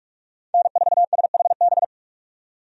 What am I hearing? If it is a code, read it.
Morse code, heard as a keyed tone, about 41 wpm, 700 Hz. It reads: N4SHB
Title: USA Call Signs - 5 Characters - 45wpm